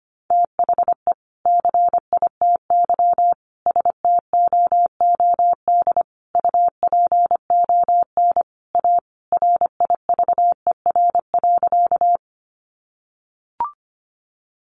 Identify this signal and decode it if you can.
Morse code, heard as a keyed tone, about 25 wpm, 700 Hz. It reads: THE CITY STOOD UPON A RIVER.
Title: Sentences from Top 1100 Words 25wpm